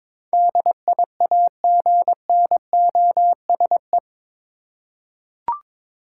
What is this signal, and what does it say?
Morse code, heard as a keyed tone, about 22 wpm, 700 Hz. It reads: DIAGNOSE